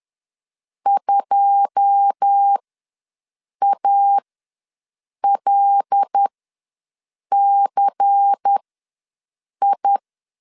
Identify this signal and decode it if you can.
Morse code, heard as a keyed tone, about 10 wpm, spaced as for 6 wpm, 800 Hz. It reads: 2ALCI
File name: LEZIONI / 9 2 ws 10wpm 4130